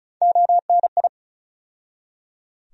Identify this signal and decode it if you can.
Morse code, heard as a keyed tone, about 34 wpm, 700 Hz. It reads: ONI
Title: USA Call Sign Postfixes - 3 Letters 35wpm